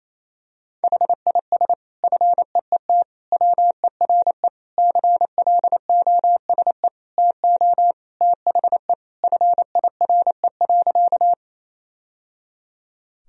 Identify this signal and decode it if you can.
Morse code, heard as a keyed tone, about 28 wpm, 700 Hz. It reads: HIS FEET WERE CLOSE TO THE FIRE.
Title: Sentences from Top 1000 Words - Review Part 1 - 28wpm